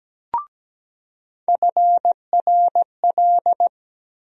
Morse code, heard as a keyed tone, about 17 wpm, 700 Hz. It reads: FRL
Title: USA Call Sign Postfixes - 3 Letters 17wpm